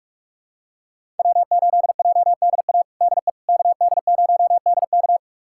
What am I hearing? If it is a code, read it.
Morse code, heard as a keyed tone, about 45 wpm, 700 Hz. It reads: W8JDA DE KD0DK
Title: USA Callsign DE Callsign - 5 Characters - 45wpm